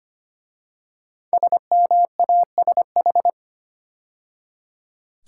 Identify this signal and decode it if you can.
Morse code, heard as a keyed tone, about 25 wpm, 700 Hz. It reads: SMASH
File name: 5 Letter Words - First 3 Letters Given - 25wpm